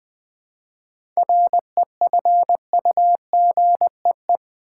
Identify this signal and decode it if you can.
Morse code, heard as a keyed tone, about 20 wpm, 700 Hz. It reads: REFUGEE
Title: 7 Letter Words - First 3 Letters Given - 20wpm